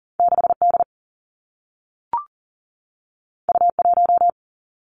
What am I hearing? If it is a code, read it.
Morse code, heard as a keyed tone, about 42 wpm, 700 Hz. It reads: 6 D U 1
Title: Two Challenging Letter-Number Combinations 40wpm